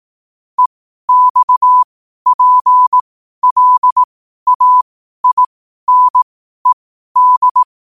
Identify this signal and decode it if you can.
Morse code, heard as a keyed tone, about 18 wpm, spaced as for 12 wpm, 1000 Hz. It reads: EXPLAINED
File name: qotdfiles / 12wpm